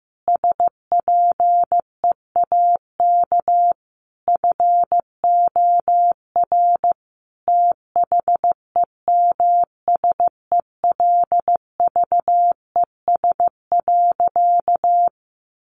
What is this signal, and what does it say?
Morse code, heard as a keyed tone, about 15 wpm, 700 Hz. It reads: SPEAK FOR THEMSELVES.
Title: Sentences from Top 1400 Words 15wpm